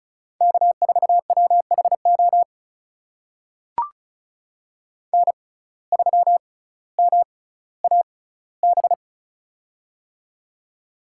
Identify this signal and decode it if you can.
Morse code, heard as a keyed tone, about 34 wpm, spaced as for 18 wpm, 700 Hz. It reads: K4WHO N3MAB
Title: USA Call Signs - 5 Characters - 6x Character Spacing - 35wpm